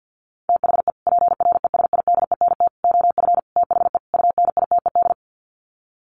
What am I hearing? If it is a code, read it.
Morse code, heard as a keyed tone, about 50 wpm, 700 Hz. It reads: THE PRESIDENT OF THE UNITED
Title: Sets of 5 Words 50wpm